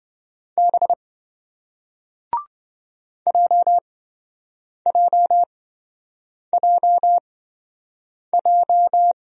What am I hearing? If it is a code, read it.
Morse code, heard as a keyed tone, about 27 wpm, spaced as for 7 wpm, 700 Hz. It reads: B JJJJ